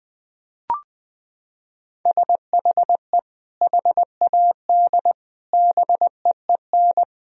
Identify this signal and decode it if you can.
Morse code, heard as a keyed tone, about 20 wpm, 700 Hz. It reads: SHE HAD BEEN
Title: Sentences from Top 1000 Words - Review Part 4 - 20wpm